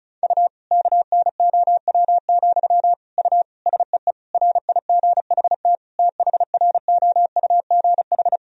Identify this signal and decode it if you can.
Morse code, heard as a keyed tone, about 34 wpm, 700 Hz. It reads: U KNOW, U SEE RIGHT THROUGH
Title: Sentences from Top 200 Words and Abbreviations 35wpm